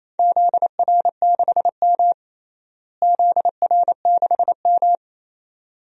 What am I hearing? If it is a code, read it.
Morse code, heard as a keyed tone, about 28 wpm, 700 Hz. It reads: ZR6M ZR6M
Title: DXCC-100 Call Signs - 4 Characters - Repeated 2x - 28wpm